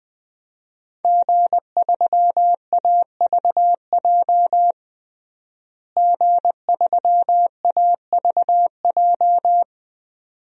Morse code, heard as a keyed tone, about 20 wpm, 700 Hz. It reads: G3AVJ G3AVJ
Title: DXCC-100 Call Signs - 5 Characters - Repeated 2x - 20wpm